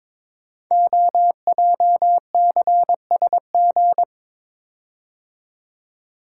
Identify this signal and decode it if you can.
Morse code, heard as a keyed tone, about 22 wpm, 700 Hz. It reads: OJCSG
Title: Code Groups - 5 Letters - 22wpm